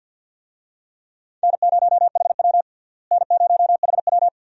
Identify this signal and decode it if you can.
Morse code, heard as a keyed tone, about 51 wpm, 700 Hz. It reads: N0HW N0HW
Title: USA Call Signs - 4 Characters - Repeated 2x - 50wpm